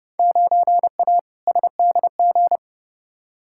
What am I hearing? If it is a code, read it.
Morse code, heard as a keyed tone, about 30 wpm, 700 Hz. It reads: 9A SDG
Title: Field Day Exchanges Encore - 30wpm